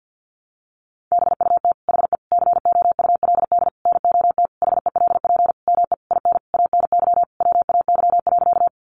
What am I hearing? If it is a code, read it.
Morse code, heard as a keyed tone, about 50 wpm, 700 Hz. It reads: BUT HE COULD NOT HELP ME IN ANY WAY.
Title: Sentences from Top 600 Words 50wpm